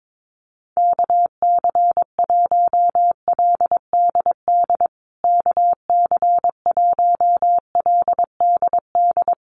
Morse code, heard as a keyed tone, about 22 wpm, 700 Hz. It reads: KC1LDD KC1LDD
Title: USA Call Signs Repeated 2x 22wpm